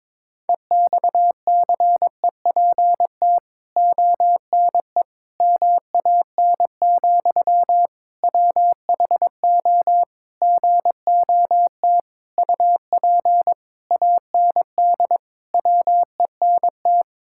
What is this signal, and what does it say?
Morse code, heard as a keyed tone, about 21 wpm, 700 Hz. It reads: EXCEPT ONE MAN, WHO GOT UP AND WENT
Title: Sentences from Top 1600 Words 22wpm